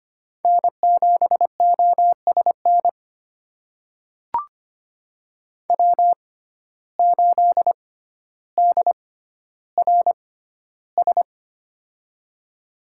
Morse code, heard as a keyed tone, about 25 wpm, spaced as for 13 wpm, 700 Hz. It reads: N7OSN W8DRS
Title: USA Call Signs - 5 Characters - 6x Character Spacing - 25wpm